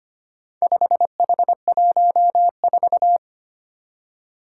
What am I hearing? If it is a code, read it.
Morse code, heard as a keyed tone, about 25 wpm, 700 Hz. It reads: 5H14
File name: practice / 25wpm / Four Letter-Number Combinations - 25wpm